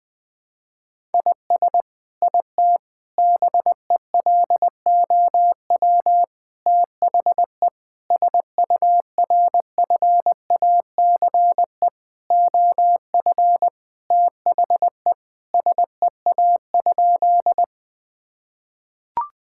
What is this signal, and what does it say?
Morse code, heard as a keyed tone, about 21 wpm, 700 Hz. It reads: IS IT BELOW THE SURFACE OF THE SEA?
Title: Sentences from Top 1000 Words - Review Part 5 - 20wpm